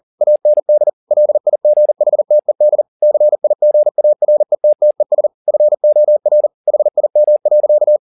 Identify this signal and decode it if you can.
Morse code, heard as a keyed tone, about 40 wpm, 600 Hz. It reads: AND LIGHTED CIGARETTES FOR HIM.
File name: the old man and the sea 2 40wpm